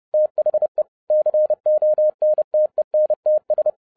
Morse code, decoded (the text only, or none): THE CONTENTS